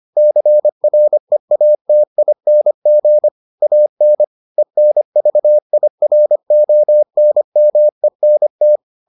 Morse code, heard as a keyed tone, about 25 wpm, 600 Hz. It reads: CREATING AN ENVIRONMENT